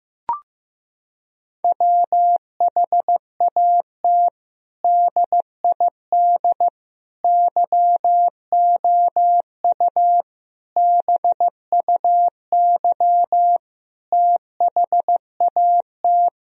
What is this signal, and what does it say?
Morse code, heard as a keyed tone, about 15 wpm, 700 Hz. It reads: WHAT DID YOU BUY THAT
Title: Sentences from Top 1000 Words - Review Part 4 - 15wpm